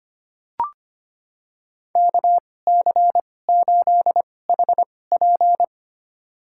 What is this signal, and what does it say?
Morse code, heard as a keyed tone, about 25 wpm, spaced as for 18 wpm, 700 Hz. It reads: KC8HP
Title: USA Call Signs - 5 Characters - 2x Character Spacing - 25wpm